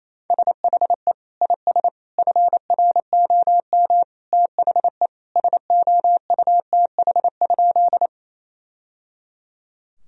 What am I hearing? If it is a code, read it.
Morse code, heard as a keyed tone, about 28 wpm, 700 Hz. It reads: SHE IS FROM THE SOUTH?
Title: Sentences from Top 1000 Words - Review Part 2 - 28wpm